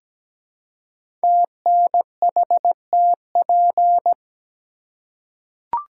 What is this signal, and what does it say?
Morse code, heard as a keyed tone, about 17 wpm, 700 Hz. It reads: TNHTP
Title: Code Groups - 5 Letters - 17wpm